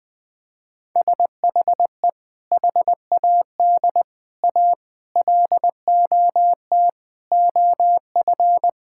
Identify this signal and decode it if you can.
Morse code, heard as a keyed tone, about 19 wpm, 700 Hz. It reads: SHE HAD A LOT OF
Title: Sentences with 4-Word Phrases - 20wpm